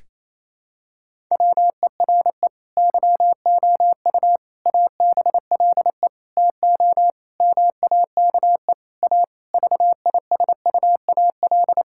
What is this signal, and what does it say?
Morse code, heard as a keyed tone, about 28 wpm, 700 Hz. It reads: WERE YOU ABLE TO MAKE A VISUAL
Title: Sentences from Top 1000 Words - Review Part 6 - 28wpm